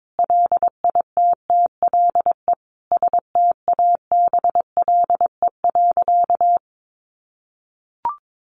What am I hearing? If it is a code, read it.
Morse code, heard as a keyed tone, about 22 wpm, 700 Hz. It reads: LITTLE STABLE.
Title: Sentences from Top 1500 Words 22wpm